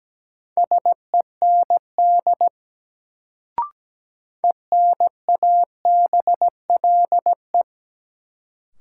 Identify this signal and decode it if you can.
Morse code, heard as a keyed tone, about 17 wpm, 700 Hz. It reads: SEND ENABLE
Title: Sentences from Top 1500 Words - New Word Drill Alt - 17wpm